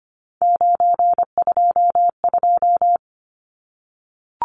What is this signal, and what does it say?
Morse code, heard as a keyed tone, about 25 wpm, 700 Hz. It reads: 922
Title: Numbers - 3-Digit 25wpm